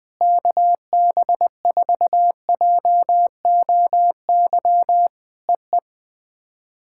Morse code, heard as a keyed tone, about 20 wpm, 700 Hz. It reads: KB4JOY EE
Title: POTA QSO Practice - 20wpm